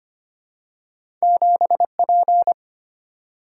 Morse code, heard as a keyed tone, about 25 wpm, 700 Hz. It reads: 7P
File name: Two Letter-Number Combinations - 25wpm